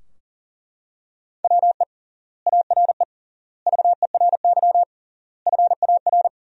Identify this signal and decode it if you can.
Morse code, heard as a keyed tone, about 40 wpm, 700 Hz. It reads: WE ARE VERY FAR
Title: 3x Word-Spacing - Sentences from Top 300 Words - 40wpm